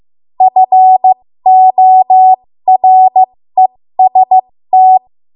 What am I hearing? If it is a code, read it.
Morse code, heard as a keyed tone, about 15 wpm, spaced as for 13 wpm, 750 Hz. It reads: FOREST
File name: Archive / 13 WPM / 13WPM